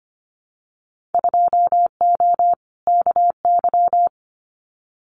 Morse code, heard as a keyed tone, about 25 wpm, 700 Hz. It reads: 2O KY
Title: Winter Field Day Exchanges 25wpm